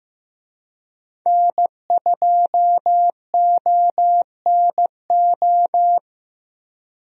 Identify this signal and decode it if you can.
Morse code, heard as a keyed tone, about 15 wpm, 700 Hz. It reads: N2ONO
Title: USA Call Signs - 5 Characters - Repeated 3x - 15wpm